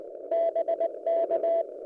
Morse code, heard as a keyed tone, about 20 wpm, 670 Hz. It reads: BK
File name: kt5e fox hunt audio 3 minutes 11-27-08